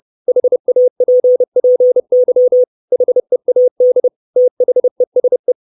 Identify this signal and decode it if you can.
Morse code, heard as a keyed tone, about 30 wpm, 500 Hz. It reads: HAPPY HEAD THESE
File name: 632 Words 30WPM